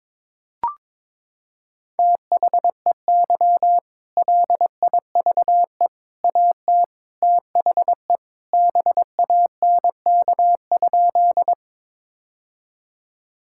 Morse code, heard as a keyed tone, about 22 wpm, 700 Hz. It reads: THEY LIVE AT THE BANK?
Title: Sentences from Top 1000 Words - Review Part 6 - 22wpm